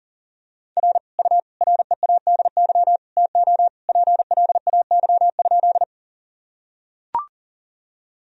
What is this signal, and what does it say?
Morse code, heard as a keyed tone, about 40 wpm, 700 Hz. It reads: R U READY TO PLAY?
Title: Sentences from Top 400 Words and Abbreviations 40wpm